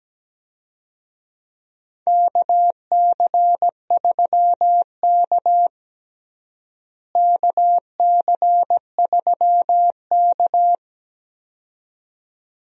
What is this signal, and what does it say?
Morse code, heard as a keyed tone, about 17 wpm, 700 Hz. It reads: KC3K KC3K
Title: USA Call Signs - 4 Characters - Repeated 2x - 17wpm